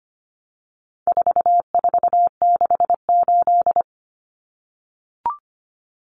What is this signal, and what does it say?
Morse code, heard as a keyed tone, about 25 wpm, 700 Hz. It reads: 4468
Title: Numbers - 4-Digit 25wpm